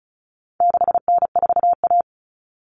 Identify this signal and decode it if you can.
Morse code, heard as a keyed tone, about 34 wpm, 700 Hz. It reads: 6N4A
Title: Common Cut Numbers - 4-Digit 35wpm